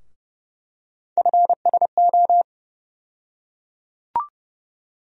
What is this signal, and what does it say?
Morse code, heard as a keyed tone, about 30 wpm, 700 Hz. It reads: FSO
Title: USA Call Sign Postfixes - 3 Letters 30wpm